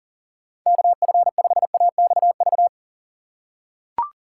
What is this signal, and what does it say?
Morse code, heard as a keyed tone, about 40 wpm, 700 Hz. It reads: KF5AXV
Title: USA Call Signs - 6 Characters - 6x Character Spacing - 40wpm